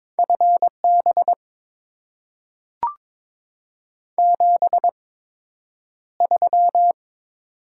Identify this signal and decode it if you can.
Morse code, heard as a keyed tone, about 22 wpm, spaced as for 9 wpm, 700 Hz. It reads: FB 73
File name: Top 20 QSO Elements - 8x Character Spacing - 22wpm